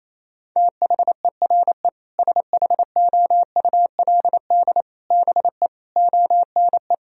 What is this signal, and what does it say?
Morse code, heard as a keyed tone, about 28 wpm, 700 Hz. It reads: THERE SHOULD BE ONE
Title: Sentences from Top 1000 Words - Review Part 3 - 28wpm